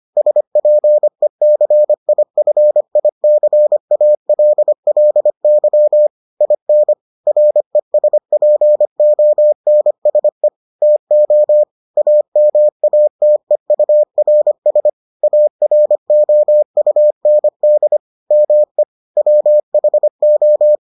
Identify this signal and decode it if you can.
Morse code, heard as a keyed tone, about 25 wpm, 600 Hz. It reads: SPECIFICALLY IN RESPONSE TO AMATEURS AROUND ME WHO